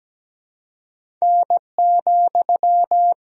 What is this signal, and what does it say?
Morse code, heard as a keyed tone, about 17 wpm, 700 Hz. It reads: N,
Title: Sentences from Top 500 Words and Abbreviations 17wpm